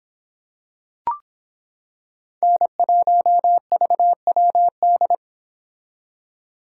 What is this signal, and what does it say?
Morse code, heard as a keyed tone, about 26 wpm, 700 Hz. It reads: N1VWD